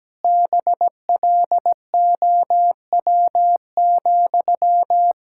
Morse code, heard as a keyed tone, about 17 wpm, 700 Hz. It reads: BLOW,
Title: Pangram Challenge v2 - 17wpm